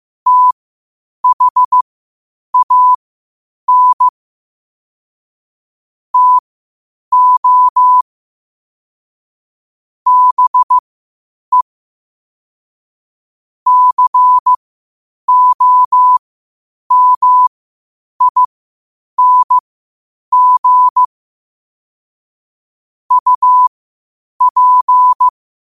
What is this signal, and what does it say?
Morse code, heard as a keyed tone, about 15 wpm, spaced as for 8 wpm, 1000 Hz. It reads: THAN TO BE COMING UP
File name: qotdfiles / 8wpm